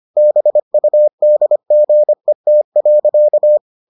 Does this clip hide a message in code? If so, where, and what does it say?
Yes: Morse code from 0.2 to 3.6 s, heard as a keyed tone, about 25 wpm, 600 Hz. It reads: BUDGET.